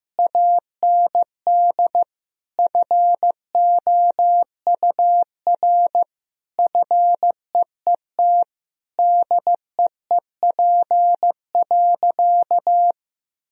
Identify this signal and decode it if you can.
Morse code, heard as a keyed tone, about 15 wpm, 700 Hz. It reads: AND FOUR FEET DEEP.
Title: Sentences from Top 1000 Words - Review Part 3 - 15wpm